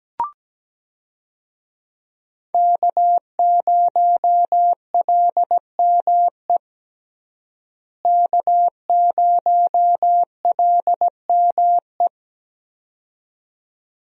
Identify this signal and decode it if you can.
Morse code, heard as a keyed tone, about 17 wpm, 700 Hz. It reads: K0LME K0LME
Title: USA Call Signs - 5 Characters - Repeated 2x - 17wpm